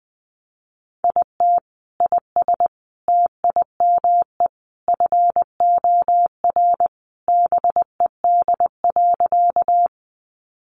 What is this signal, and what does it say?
Morse code, heard as a keyed tone, about 20 wpm, 700 Hz. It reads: IT IS TIME FOR BED.
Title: Sentences from Top 500 Words 20wpm